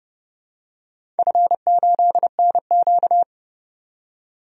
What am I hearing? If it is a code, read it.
Morse code, heard as a keyed tone, about 32 wpm, 700 Hz. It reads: F8NQ
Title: DXCC-100 Call Signs - 4 Characters - 30wpm